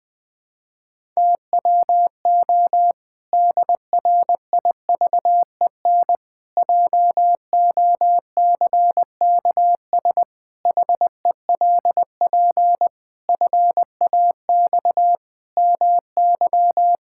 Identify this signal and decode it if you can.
Morse code, heard as a keyed tone, about 21 wpm, 700 Hz. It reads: TWO DRIVEN JOCKS HELP FAX MY
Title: Pangram Challenge v2 - 20wpm